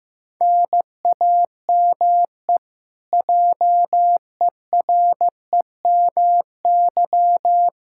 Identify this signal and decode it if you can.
Morse code, heard as a keyed tone, about 15 wpm, 700 Hz. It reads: NAME JEREMY
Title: Practice QSO - Exchange 1 - Beginning - 15wpm